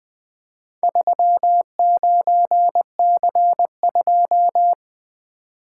Morse code, heard as a keyed tone, about 20 wpm, 700 Hz. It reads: 39C2
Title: Four Letter-Number Combinations - 20wpm